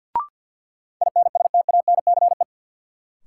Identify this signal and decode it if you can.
Morse code, heard as a keyed tone, about 53 wpm, 700 Hz. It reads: INSTANCE
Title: Sentences from Top 800 Words - New Word Drill - 50wpm